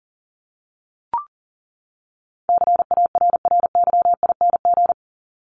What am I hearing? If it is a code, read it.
Morse code, heard as a keyed tone, about 40 wpm, 700 Hz. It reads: CARRYING
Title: ING Suffix - Top 100 Words with Base Words 40wpm